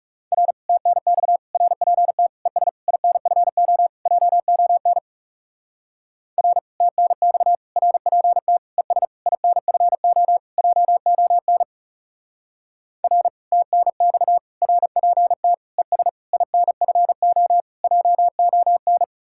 Morse code, heard as a keyed tone, about 38 wpm, 700 Hz. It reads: R TNX RPT ES INFO JON R TNX RPT ES INFO JON R TNX RPT ES INFO JON